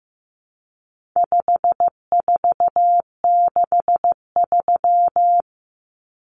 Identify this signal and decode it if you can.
Morse code, heard as a keyed tone, about 15 wpm, 700 Hz. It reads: 5463
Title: Numbers - 4-Digit 15wpm